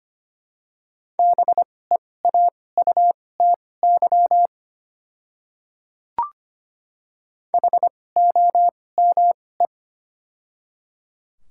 Morse code, heard as a keyed tone, about 25 wpm, spaced as for 18 wpm, 700 Hz. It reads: BEAUTY HOME
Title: 2x Character Spacing - Top 500 Words - 25wpm